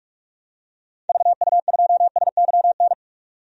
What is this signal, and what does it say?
Morse code, heard as a keyed tone, about 46 wpm, 700 Hz. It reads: VU2SYN
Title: DXCC-100 Call Signs - 6 Characters - 45wpm